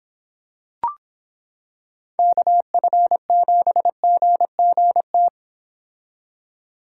Morse code, heard as a keyed tone, about 25 wpm, 700 Hz. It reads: KF7GGT